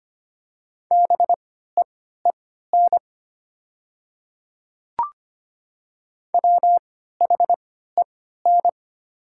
Morse code, heard as a keyed tone, about 25 wpm, spaced as for 14 wpm, 700 Hz. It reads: BEEN WHEN
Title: 3x Character Spacing - Top 300 Words - 25wpm